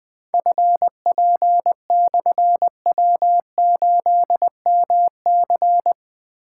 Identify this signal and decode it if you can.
Morse code, heard as a keyed tone, about 20 wpm, 700 Hz. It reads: FP/W8MC
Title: Difficult Call Signs 20wpm